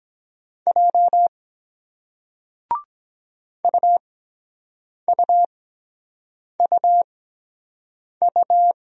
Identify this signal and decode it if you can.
Morse code, heard as a keyed tone, about 23 wpm, spaced as for 7 wpm, 700 Hz. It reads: J UUUU